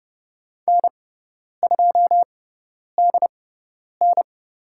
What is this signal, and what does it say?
Morse code, heard as a keyed tone, about 30 wpm, spaced as for 10 wpm, 700 Hz. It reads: N2DN
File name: Lesson 16g - Letter c - Call Signs 30wpm